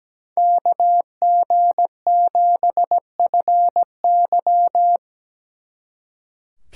Morse code, heard as a keyed tone, about 20 wpm, 700 Hz. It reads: KG7FY
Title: Speed Racing - USA 2x2 Call Signs 17wpm